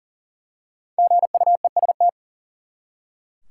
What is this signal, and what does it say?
Morse code, heard as a keyed tone, about 40 wpm, 700 Hz. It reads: GUEST
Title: Five Letter Words - 40wpm